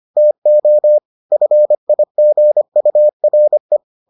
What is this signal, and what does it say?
Morse code, heard as a keyed tone, about 25 wpm, 600 Hz. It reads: TO FIGURE